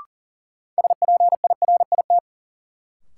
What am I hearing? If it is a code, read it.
Morse code, heard as a keyed tone, about 40 wpm, 700 Hz. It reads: SPIRIT
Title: Sentences from Top 1400 Words - New Word Drill Alt - 40wpm